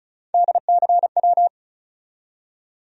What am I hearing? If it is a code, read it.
Morse code, heard as a keyed tone, about 36 wpm, 700 Hz. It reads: DCW
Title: USA Call Sign Postfixes - 3 Letters 35wpm